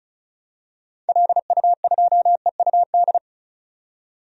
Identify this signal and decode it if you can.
Morse code, heard as a keyed tone, about 35 wpm, 700 Hz. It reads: LU2EUD
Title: DXCC-100 Call Signs - 6 Characters - 35wpm